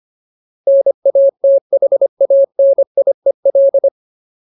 Morse code, heard as a keyed tone, about 25 wpm, 550 Hz. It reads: NATHANIEL
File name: Ultimate Word List / 25WPM Ultimate